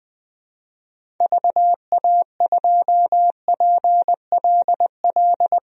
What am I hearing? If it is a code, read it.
Morse code, heard as a keyed tone, about 20 wpm, 700 Hz. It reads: VA2PLL
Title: DXCC-100 Call Signs - 6 Characters - 20wpm